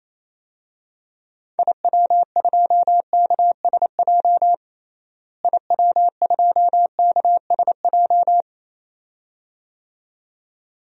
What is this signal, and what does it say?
Morse code, heard as a keyed tone, about 28 wpm, 700 Hz. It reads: IW2KSJ IW2KSJ
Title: DXCC-100 Call Signs - 6 Characters - Repeated 2x - 28wpm